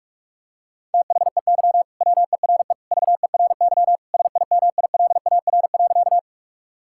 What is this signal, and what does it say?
Morse code, heard as a keyed tone, about 43 wpm, 700 Hz. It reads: THEY WERE VERY SIMILAR.
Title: Sentences from Top 1000 Words - Review Part 5 - 45wpm